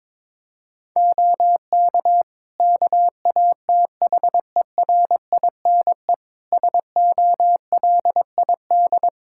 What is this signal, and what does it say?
Morse code, heard as a keyed tone, about 22 wpm, 700 Hz. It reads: OK KATHERINE SOLID
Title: K1USN SST QSO Practice 22wpm